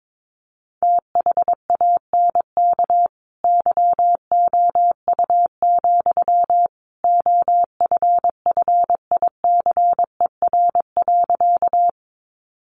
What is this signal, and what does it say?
Morse code, heard as a keyed tone, about 22 wpm, 700 Hz. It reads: THANK YOU, OFFICER.
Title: Sentences from Top 1200 Words 22wpm